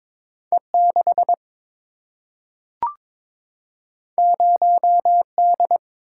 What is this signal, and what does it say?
Morse code, heard as a keyed tone, about 22 wpm, 700 Hz. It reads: E 6 0 D
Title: Single-Letter Single-Number 22wpm